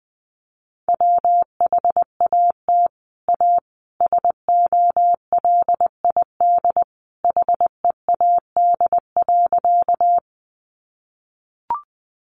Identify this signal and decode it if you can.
Morse code, heard as a keyed tone, about 20 wpm, 700 Hz. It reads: WHAT A SOLID HEAD.
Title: Sentences from Top 1500 Words 20wpm